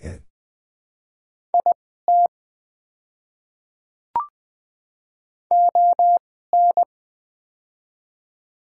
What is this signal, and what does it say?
Morse code, heard as a keyed tone, about 21 wpm, spaced as for 15 wpm, 700 Hz. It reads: IT ON
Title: 2x Character Spacing - Top 100 Words - 20wpm